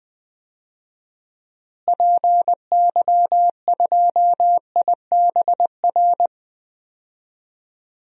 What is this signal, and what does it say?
Morse code, heard as a keyed tone, about 20 wpm, 700 Hz. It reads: PY2IBR
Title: DXCC-100 Call Signs - 6 Characters - Repeated 2x - 20wpm